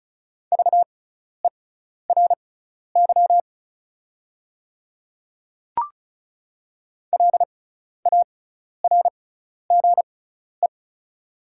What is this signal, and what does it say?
Morse code, heard as a keyed tone, about 36 wpm, spaced as for 12 wpm, 690 Hz. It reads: VERY LARGE
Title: 6x Character Spacing - Top 300 Words - 35wpm